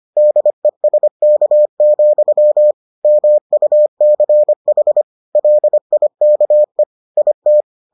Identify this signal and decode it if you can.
Morse code, heard as a keyed tone, about 25 wpm, 600 Hz. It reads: DESK, MUCH LIKE IT